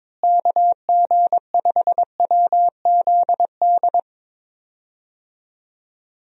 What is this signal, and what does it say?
Morse code, heard as a keyed tone, about 22 wpm, 700 Hz. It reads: KG5WZD
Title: USA Call Signs - 6 Characters - Repeated 2x - 22wpm